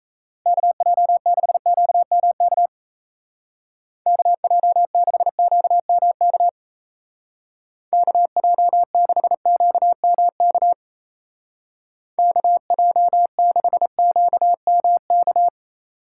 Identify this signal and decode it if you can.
Morse code, heard as a keyed tone, about 34 wpm, 700 Hz. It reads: KJ6QMK KJ6QMK KJ6QMK KJ6QMK
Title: Speed Racing - USA 2x3 Call Signs 28wpm